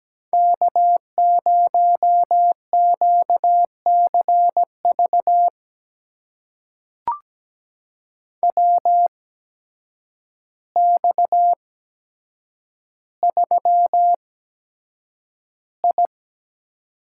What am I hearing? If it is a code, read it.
Morse code, heard as a keyed tone, about 17 wpm, spaced as for 8 wpm, 700 Hz. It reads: K0QCV WX3I
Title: USA Call Signs - 8x Character Spacing - 17wpm